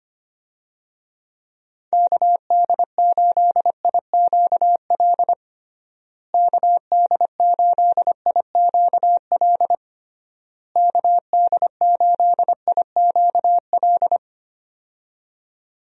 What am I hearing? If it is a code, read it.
Morse code, heard as a keyed tone, about 25 wpm, 700 Hz. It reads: KD8IQL KD8IQL KD8IQL
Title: USA Call Signs - 6 Characters - Repeated 3x - 25wpm